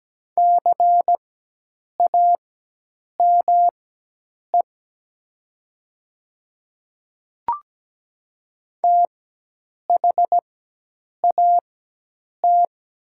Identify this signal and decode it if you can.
Morse code, heard as a keyed tone, about 17 wpm, spaced as for 8 wpm, 690 Hz. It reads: CAME THAT
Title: 4x Character Spacing - Top 300 Words - 17wpm